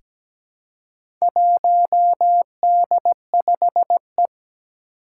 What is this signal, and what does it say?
Morse code, heard as a keyed tone, about 17 wpm, 700 Hz. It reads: 1D5E
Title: Four Challenging Letter-Number Combinations 17wpm